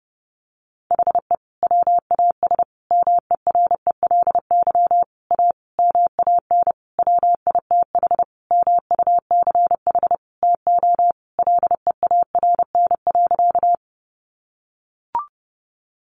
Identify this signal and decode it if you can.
Morse code, heard as a keyed tone, about 30 wpm, 700 Hz. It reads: HE WAS MERELY A MAN WITH MUCH TO LEARN.
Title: Sentences from Top 1400 Words 30wpm